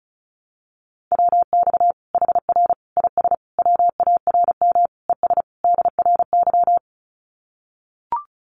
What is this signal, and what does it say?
Morse code, heard as a keyed tone, about 34 wpm, 700 Hz. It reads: WX HR IS WARM ES DRY
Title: Common QSO Phrases 35wpm